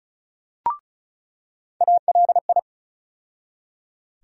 Morse code, heard as a keyed tone, about 36 wpm, 700 Hz. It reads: ALI
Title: USA Call Sign Postfixes - 3 Letters 35wpm